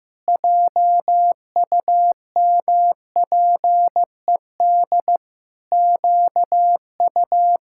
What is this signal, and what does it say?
Morse code, heard as a keyed tone, about 15 wpm, 700 Hz. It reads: JUMPED QU
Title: Pangram Challenge v2 - 15wpm